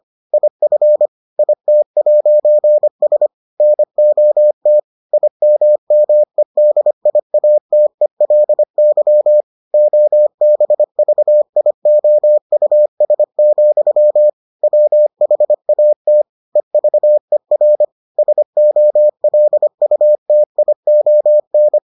Morse code, heard as a keyed tone, about 25 wpm, 600 Hz. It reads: IF IT'S NOT IMMEDIATELY OBVIOUS, WHATEVER SOLUTION